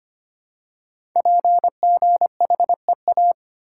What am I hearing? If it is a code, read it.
Morse code, heard as a keyed tone, about 25 wpm, 700 Hz. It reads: PGHEA
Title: Code Groups - 5 Letters - 25wpm